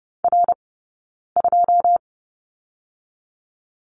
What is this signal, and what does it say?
Morse code, heard as a keyed tone, about 30 wpm, 700 Hz. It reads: R 2
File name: Lesson 18d - Letter m - Two Characters 30wpm